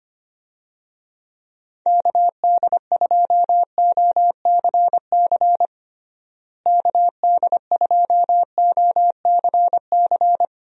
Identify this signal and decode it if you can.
Morse code, heard as a keyed tone, about 25 wpm, 700 Hz. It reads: KD2OCC KD2OCC
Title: USA Call Signs - 6 Characters - Repeated 2x - 25wpm